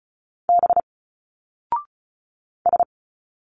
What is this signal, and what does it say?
Morse code, heard as a keyed tone, about 34 wpm, 700 Hz. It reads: B S
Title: Single Letters 35wpm